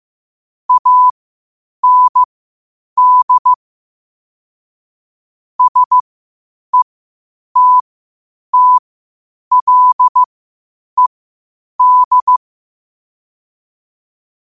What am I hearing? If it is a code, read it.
Morse code, heard as a keyed tone, about 15 wpm, spaced as for 8 wpm, 1000 Hz. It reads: AND SETTLED
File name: qotdfiles / 8wpm